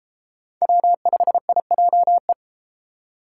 Morse code, heard as a keyed tone, about 32 wpm, 700 Hz. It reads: W5IJE